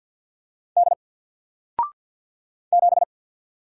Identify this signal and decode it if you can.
Morse code, heard as a keyed tone, about 49 wpm, 700 Hz. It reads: D 7
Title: Single Letter-Number 50wpm